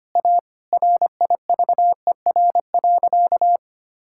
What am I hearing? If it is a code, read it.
Morse code, heard as a keyed tone, about 25 wpm, 700 Hz. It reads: A RIVER.
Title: Sentences from Top 1000 Words - Review Part 6 - 25wpm